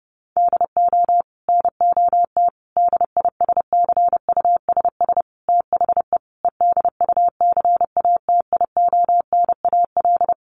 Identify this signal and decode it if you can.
Morse code, heard as a keyed tone, about 30 wpm, 700 Hz. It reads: DO NOT DISCUSS THE EDUCATIONAL
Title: Sentences from Top 1200 Words 30wpm